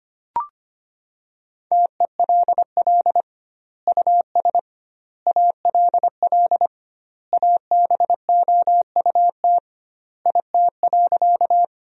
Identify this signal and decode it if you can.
Morse code, heard as a keyed tone, about 25 wpm, 700 Hz. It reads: TELL US ALL ABOUT IT.
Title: 2x Word-Spacing - Sentences from Top 200 Words - 25wpm